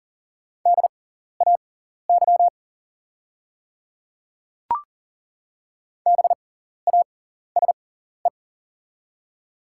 Morse code, heard as a keyed tone, about 41 wpm, spaced as for 14 wpm, 690 Hz. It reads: DAY BASE
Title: 6x Character Spacing - Top 300 Words - 40wpm